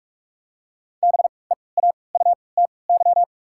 Morse code, heard as a keyed tone, about 43 wpm, spaced as for 25 wpm, 700 Hz. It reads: BEAUTY
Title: 3x Character Spacing - Top 500 Words - 45wpm